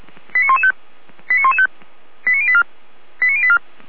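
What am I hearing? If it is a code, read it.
Five-tone selective call (ZVEI), heard as quick runs of five tones, tones of about 80 ms. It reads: 89115 89115 89084 89084
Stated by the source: Five-tone ZVEI selective call 89223 received by SDR, on the feuerwehr-orlamuende.org server.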